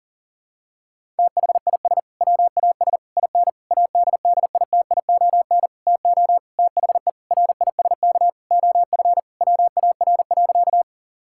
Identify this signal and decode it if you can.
Morse code, heard as a keyed tone, about 40 wpm, 700 Hz. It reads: THIS WAS IN ADDITION TO THE RISK OF WAR.
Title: Sentences from Top 1000 Words - Review Part 6 - 40wpm